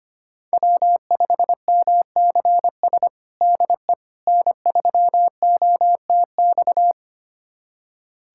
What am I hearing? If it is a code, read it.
Morse code, heard as a keyed tone, about 25 wpm, 700 Hz. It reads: W5MCS DE N3OTX
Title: USA Callsign DE Callsign - 5 Characters - 25wpm